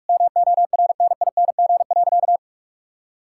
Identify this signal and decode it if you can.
Morse code, heard as a keyed tone, about 46 wpm, 700 Hz. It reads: MORNING.
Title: Sentences from Top 1000 Words - Review Part 5 - 45wpm